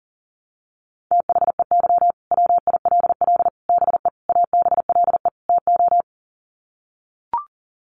Morse code, heard as a keyed tone, about 40 wpm, 700 Hz. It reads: THEY WILL BE ABLE TO
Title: Sets of 5 Words Encore 40wpm